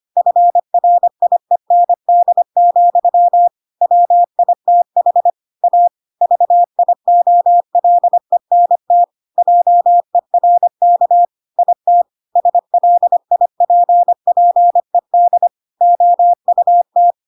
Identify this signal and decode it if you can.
Morse code, heard as a keyed tone, about 25 wpm, 700 Hz. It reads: FRIEND, WITH A VIOLENT JERK IT SLIPPED OUT
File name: Pinocchio / Pinocchio ch2 25 25 ik1hge